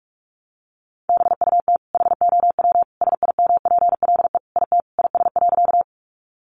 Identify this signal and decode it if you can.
Morse code, heard as a keyed tone, about 46 wpm, 700 Hz. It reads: BUT HOW SIMPLE IT IS.
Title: Sentences from Top 400 Words 45wpm